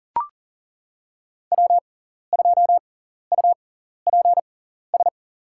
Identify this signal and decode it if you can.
Morse code, heard as a keyed tone, about 40 wpm, spaced as for 14 wpm, 700 Hz. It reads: W2UPS
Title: USA Call Signs - 6x Character Spacing - 40wpm